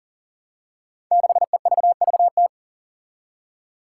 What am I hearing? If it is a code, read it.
Morse code, heard as a keyed tone, about 40 wpm, 700 Hz. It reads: 6EVVT
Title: Five Challenging Letter-Number Combinations 40wpm